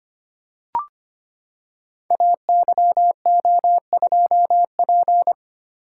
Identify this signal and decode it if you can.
Morse code, heard as a keyed tone, about 25 wpm, 700 Hz. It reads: AYO2P
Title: Code Groups - 5 Letter:Numbers - 25wpm